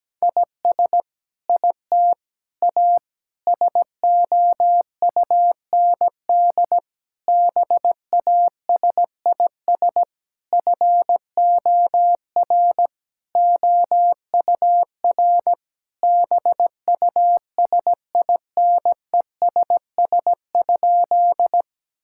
Morse code, heard as a keyed tone, about 17 wpm, 700 Hz. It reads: IS IT A SOUND BASIS FOR OUR BUSINESS?
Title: Sentences from Top 1000 Words - Review Part 5 - 17wpm